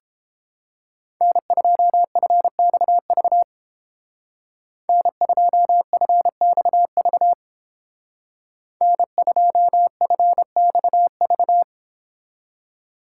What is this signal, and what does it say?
Morse code, heard as a keyed tone, about 31 wpm, 700 Hz. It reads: N2FXV N2FXV N2FXV